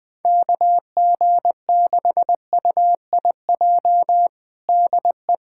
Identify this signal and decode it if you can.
Morse code, heard as a keyed tone, about 21 wpm, 700 Hz. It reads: KG6UIJ DE
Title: USA Callsign DE Callsign - 6 Characters - 20wpm